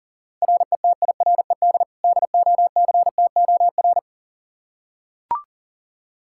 Morse code, heard as a keyed tone, about 40 wpm, 700 Hz. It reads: RETIRED DOCTOR
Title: Practice QSO - Exchange 3 - More Info - 40wpm